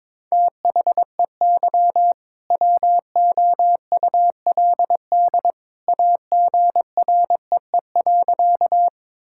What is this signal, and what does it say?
Morse code, heard as a keyed tone, about 21 wpm, 700 Hz. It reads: THEY WOULD AGREE.
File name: practice / 22wpm / Sentences with 2-Word Phrases - 22wpm